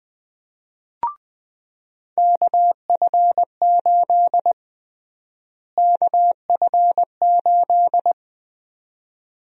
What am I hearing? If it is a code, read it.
Morse code, heard as a keyed tone, about 21 wpm, 700 Hz. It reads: KF8 KF8
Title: USA Call Sign Prefixes - Repeated 2x - 20wpm